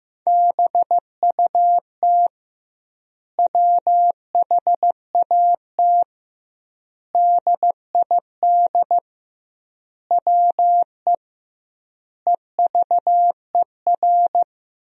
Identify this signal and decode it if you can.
Morse code, heard as a keyed tone, about 16 wpm, 700 Hz. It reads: BUT WHAT DID WE EVER
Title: 2x Word-Spacing - Sentences from Top 400 Words - 15wpm